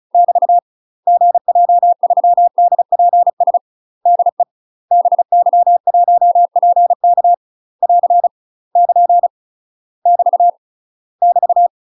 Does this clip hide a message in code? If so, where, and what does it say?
Yes: Morse code from 0.1 to 11.7 s, heard as a keyed tone, about 35 wpm, 700 Hz. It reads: <BT> GJ3DPS DE BY1PK <AR> <KN> <BT> <BT>